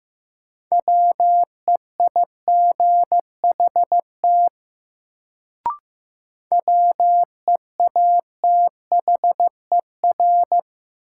Morse code, heard as a keyed tone, about 15 wpm, 700 Hz. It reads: WEIGHT WEATHER
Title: Sentences from Top 800 Words - New Word Drill - 15wpm